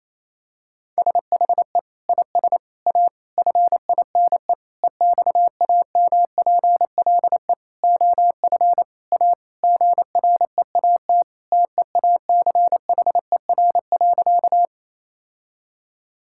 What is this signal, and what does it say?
Morse code, heard as a keyed tone, about 28 wpm, 700 Hz. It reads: SHE IS A FINE EXAMPLE OF A GREAT TEACHER.
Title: Sentences from Top 1000 Words - Review Part 6 - 28wpm